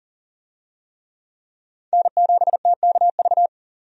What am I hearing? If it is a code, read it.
Morse code, heard as a keyed tone, about 40 wpm, 700 Hz. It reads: N7TKV
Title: USA Call Signs - 5 Characters - 40wpm